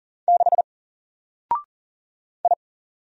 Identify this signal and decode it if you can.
Morse code, heard as a keyed tone, about 40 wpm, 700 Hz. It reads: 6 I
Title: Single Character (Letter:Number:Punctuation:Prosign) Alt 40wpm